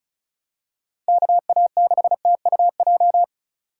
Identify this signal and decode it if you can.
Morse code, heard as a keyed tone, about 34 wpm, 700 Hz. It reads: KA6TUJ
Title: USA Call Signs - 6 Characters - 3x Character Spacing - 35wpm